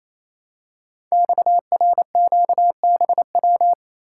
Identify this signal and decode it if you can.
Morse code, heard as a keyed tone, about 28 wpm, 700 Hz. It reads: XRQBW
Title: Code Groups - 5 Letters - 28wpm